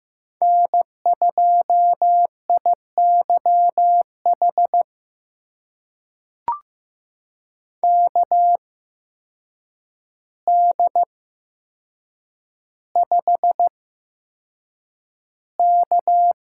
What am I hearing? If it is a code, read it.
Morse code, heard as a keyed tone, about 15 wpm, spaced as for 7 wpm, 700 Hz. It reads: N2IYH KD5K
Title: USA Call Signs - 8x Character Spacing - 15wpm